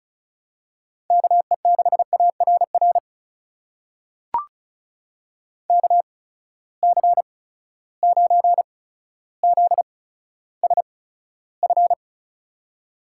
Morse code, heard as a keyed tone, about 36 wpm, spaced as for 15 wpm, 700 Hz. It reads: KE6ARR KC9ZSF
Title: USA Call Signs - 6 Characters - 8x Character Spacing - 35wpm